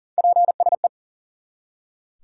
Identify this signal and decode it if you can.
Morse code, heard as a keyed tone, about 40 wpm, 700 Hz. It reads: PSE
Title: Top 90 QSO Elements 40wpm